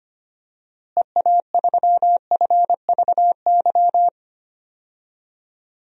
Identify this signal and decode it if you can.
Morse code, heard as a keyed tone, about 25 wpm, 700 Hz. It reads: EA3FVY
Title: DXCC-100 Call Signs - 6 Characters - Repeated 3x - 25wpm